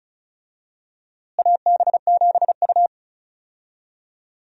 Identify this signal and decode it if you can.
Morse code, heard as a keyed tone, about 36 wpm, 700 Hz. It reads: AB7U
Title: USA Call Signs - 4 Characters - Repeated 3x - 35wpm